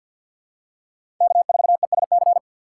Morse code, heard as a keyed tone, about 46 wpm, 700 Hz. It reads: K4ESC
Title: USA Call Signs - 5 Characters - 3x Character Spacing - 50wpm